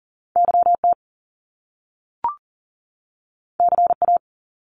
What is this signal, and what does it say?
Morse code, heard as a keyed tone, about 42 wpm, 700 Hz. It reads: YT CA
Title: Top Two Letter Combinations in Words 40wpm